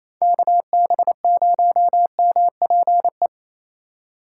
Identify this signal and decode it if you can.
Morse code, heard as a keyed tone, about 28 wpm, 700 Hz. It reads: KB0MPE
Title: USA Call Signs - 6 Characters - 3x Character Spacing - 28wpm